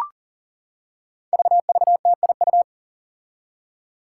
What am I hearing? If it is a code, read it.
Morse code, heard as a keyed tone, about 40 wpm, 700 Hz. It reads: VVTIU
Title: Five Challenging Letter-Number Combinations 40wpm